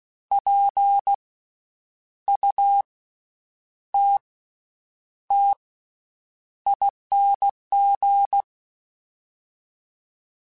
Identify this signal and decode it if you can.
Morse code, heard as a keyed tone, about 16 wpm, 800 Hz. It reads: P U T T ING
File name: Special INGs-mp3s / Special INGs-mp3s / Slow ing @15